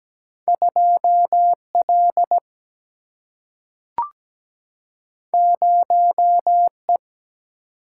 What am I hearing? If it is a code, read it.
Morse code, heard as a keyed tone, about 17 wpm, 700 Hz. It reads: 2 L 0 E
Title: Single-Letter Single-Number 17wpm